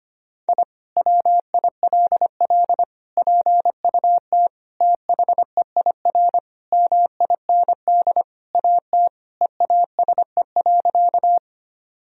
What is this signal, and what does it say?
Morse code, heard as a keyed tone, about 25 wpm, 700 Hz. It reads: I WILL PUT THEIR MIND AT EASE.
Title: Sentences from Top 1000 Words - Review Part 1 - 25wpm